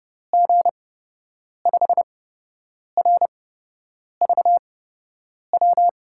Morse code, heard as a keyed tone, about 30 wpm, spaced as for 8 wpm, 700 Hz. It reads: G5RVW
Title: USA Call Signs - 6 Characters - 8x Character Spacing - 30wpm